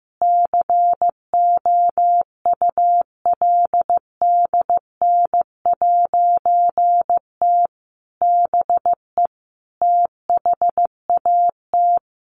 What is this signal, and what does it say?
Morse code, heard as a keyed tone, about 15 wpm, 700 Hz. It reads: COULDN'T BE THAT
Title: A Sound Of Thunder 15wpm